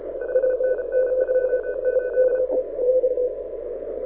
Morse code, heard as a keyed tone, about 33 wpm, 1500 Hz, with radio noise. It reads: 5NN 28
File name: audio zawody / IARU 2015 CW SN7Q